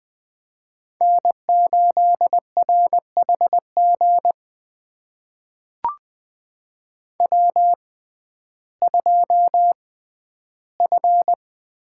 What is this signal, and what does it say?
Morse code, heard as a keyed tone, about 20 wpm, spaced as for 12 wpm, 700 Hz. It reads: N8RHG W2F
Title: USA Call Signs - 5 Characters - 6x Character Spacing - 20wpm